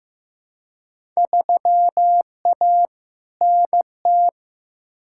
Morse code, heard as a keyed tone, about 15 wpm, 700 Hz. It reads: 3A NT
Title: Field Day Exchanges Encore - 15wpm